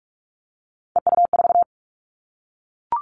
Morse code, heard as a keyed tone, about 44 wpm, 700 Hz. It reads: EU4
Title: Callsign Prefixes for Top 100 DXCC - 45wpm